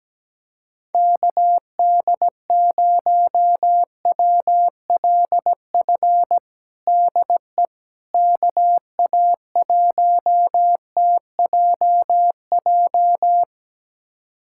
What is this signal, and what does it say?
Morse code, heard as a keyed tone, about 17 wpm, 700 Hz. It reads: KD0WLF DE KA1TJJ
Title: USA Callsign DE Callsign - 6 Characters - 17wpm